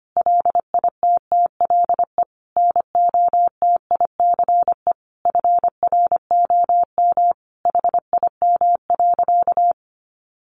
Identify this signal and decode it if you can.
Morse code, heard as a keyed tone, about 25 wpm, 700 Hz. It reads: LITTLE NOTICE FROM HIM.
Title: Sentences from Top 400 Words 25wpm